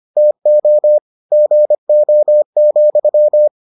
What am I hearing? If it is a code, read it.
Morse code, heard as a keyed tone, about 25 wpm, 600 Hz. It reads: TO GO,